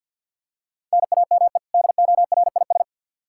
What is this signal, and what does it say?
Morse code, heard as a keyed tone, about 46 wpm, 700 Hz. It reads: NAME DORIS
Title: Practice QSO - Exchange 1 - Beginning - 50wpm